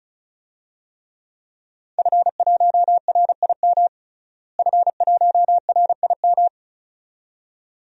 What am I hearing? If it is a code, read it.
Morse code, heard as a keyed tone, about 36 wpm, 700 Hz. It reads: F1RIM F1RIM
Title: DXCC-100 Call Signs - 5 Characters - Repeated 2x - 35wpm